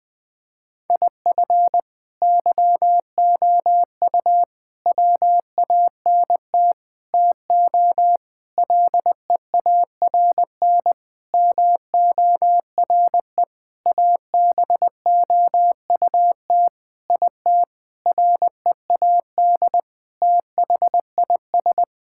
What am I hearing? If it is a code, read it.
Morse code, heard as a keyed tone, about 20 wpm, 700 Hz. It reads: IF YOU WANT TO LEARN MORE ABOUT IT READ THIS
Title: Sentences with 7-Word Phrases - 20wpm